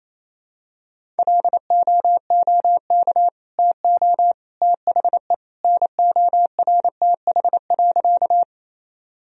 Morse code, heard as a keyed tone, about 28 wpm, 700 Hz. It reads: LOOK TO THE NORTH.
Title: Sentences from Top 1000 Words - Review Part 1 - 28wpm